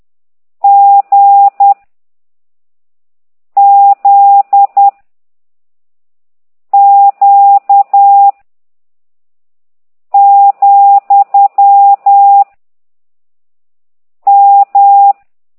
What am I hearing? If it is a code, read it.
Morse code, heard as a keyed tone, about 10 wpm, 800 Hz. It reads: G Z Q , M